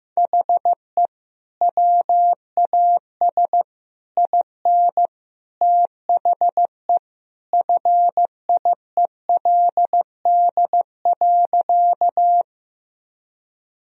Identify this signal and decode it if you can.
Morse code, heard as a keyed tone, about 15 wpm, 700 Hz. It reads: HE WAS IN THE FIELD.
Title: Sentences from Top 1000 Words - Review Part 2 - 15wpm